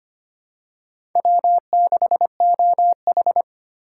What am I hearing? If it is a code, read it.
Morse code, heard as a keyed tone, about 25 wpm, 700 Hz. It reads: W6OH
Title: USA Call Signs - 4 Characters - 2x Character Spacing - 25wpm